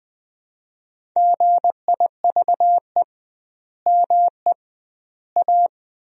Morse code, heard as a keyed tone, about 20 wpm, 700 Hz. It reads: GIVE ME A
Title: 2x Word-Spacing - Sentences from Top 200 Words - 20wpm